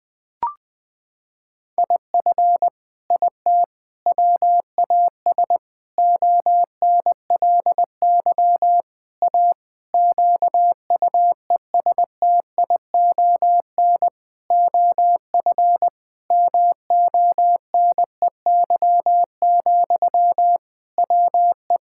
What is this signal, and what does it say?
Morse code, heard as a keyed tone, about 20 wpm, 700 Hz. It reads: IF IT WAS ONLY A QUESTION OF MONEY, WE
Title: Sentences from Top 1600 Words 20wpm